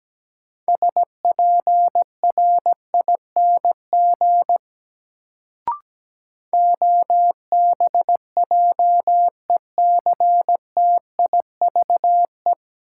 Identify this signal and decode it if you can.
Morse code, heard as a keyed tone, about 17 wpm, 700 Hz. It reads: SPRING OBJECTIVE
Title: Sentences from Top 1200 Words - New Word Drill Alt - 17wpm